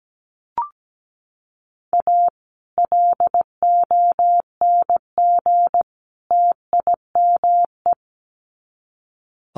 Morse code, heard as a keyed tone, about 17 wpm, 700 Hz. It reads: A LONG TIME
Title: Sets of 3 Words 17wpm